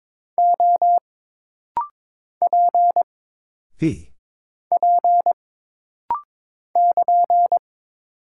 Morse code, heard as a keyed tone, about 22 wpm, 700 Hz. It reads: O P P <KN>
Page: This rapid-fire timing is geared to push your ICR (Instant Character Recognition) skills.